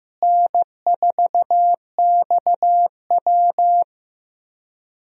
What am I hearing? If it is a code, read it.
Morse code, heard as a keyed tone, about 15 wpm, 700 Hz. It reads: N4XW
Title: USA Call Signs - 4 Characters - Repeated 2x - 15wpm